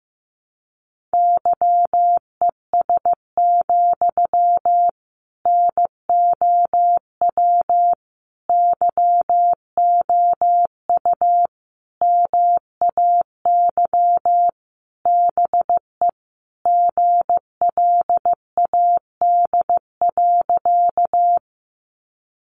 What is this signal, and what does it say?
Morse code, heard as a keyed tone, about 14 wpm, 700 Hz. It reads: YES, NOW YOU MAY BE GLAD.
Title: Sentences from Top 800 Words 15wpm